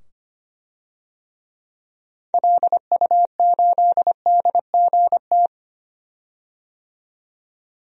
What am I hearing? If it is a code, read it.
Morse code, heard as a keyed tone, about 25 wpm, 700 Hz. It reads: LU8DGT
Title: DXCC-100 Call Signs - 6 Characters - 25wpm